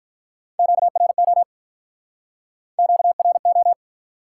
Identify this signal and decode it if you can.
Morse code, heard as a keyed tone, about 48 wpm, 700 Hz. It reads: QRO QRO